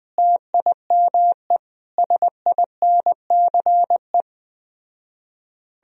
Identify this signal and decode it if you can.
Morse code, heard as a keyed tone, about 20 wpm, 700 Hz. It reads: TIME SINCE
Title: Speed-Racing - Sets of 5 Words 17wpm